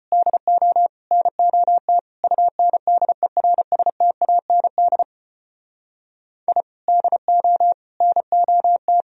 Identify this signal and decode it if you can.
Morse code, heard as a keyed tone, about 31 wpm, 700 Hz. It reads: DO NOT UNDERSTAND I DO NOT